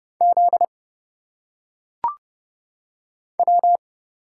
Morse code, heard as a keyed tone, about 30 wpm, 700 Hz. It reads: Z W
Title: Single Letters - Phonetic Pronunciation 30wpm